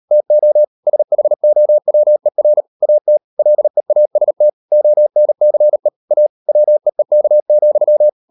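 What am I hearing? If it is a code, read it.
Morse code, heard as a keyed tone, about 38 wpm, 600 Hz. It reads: TO SHOWER AT LEAST ONCE A WEEK,
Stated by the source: Morse code practice at 38 WPM, sending Ken Follett's novel Never in English, from CW med Gnister.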